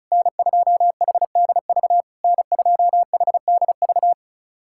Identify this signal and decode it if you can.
Morse code, heard as a keyed tone, about 34 wpm, 690 Hz. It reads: N2HDV N2HDV
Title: SOTA QSO Practice 35wpm